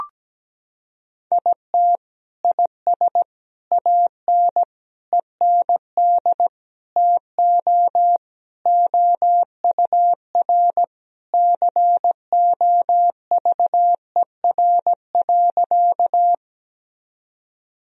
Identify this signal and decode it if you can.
Morse code, heard as a keyed tone, about 17 wpm, 700 Hz. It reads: IT IS AN END TO OUR COVER.
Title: Sentences from Top 1000 Words - Review Part 1 - 17wpm